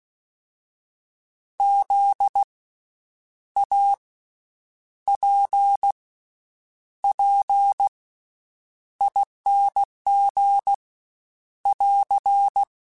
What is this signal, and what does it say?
Morse code, heard as a keyed tone, about 16 wpm, 800 Hz. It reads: Z A P P ING <AR>
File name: Special INGs-mp3s / Special INGs-mp3s / Slow ing @15